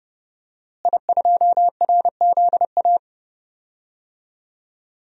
Morse code, heard as a keyed tone, about 30 wpm, 700 Hz. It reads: I2RZA
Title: DXCC-100 Call Signs - 5 Characters - Repeated 2x - 30wpm